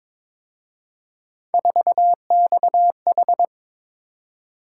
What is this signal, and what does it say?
Morse code, heard as a keyed tone, about 26 wpm, 700 Hz. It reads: 4XH
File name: Speed-Racing - Three Letter-Number Combinations - 22wpm